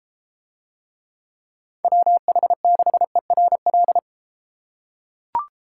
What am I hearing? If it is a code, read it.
Morse code, heard as a keyed tone, about 30 wpm, 700 Hz. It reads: WH6ERL